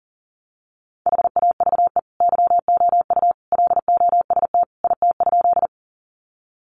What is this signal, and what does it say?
Morse code, heard as a keyed tone, about 40 wpm, 700 Hz. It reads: HAVE YOU LOST IT?
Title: Sentences from Top 1400 Words 40wpm